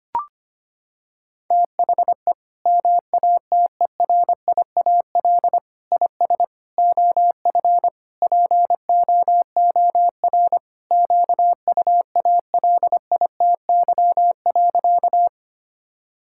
Morse code, heard as a keyed tone, about 25 wpm, 700 Hz. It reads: THE MATERIAL IS OF POOR QUALITY.
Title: Sentences from Top 1000 Words - Review Part 5 - 25wpm